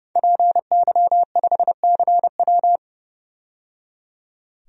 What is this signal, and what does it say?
Morse code, heard as a keyed tone, about 31 wpm, 700 Hz. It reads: PY5CW
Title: DXCC-100 Call Signs - 5 Characters - 30wpm